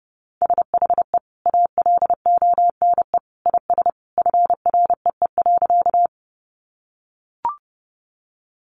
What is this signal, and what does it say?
Morse code, heard as a keyed tone, about 30 wpm, 700 Hz. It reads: SHE ALONE IS FREE.
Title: Sentences from Top 1000 Words 30wpm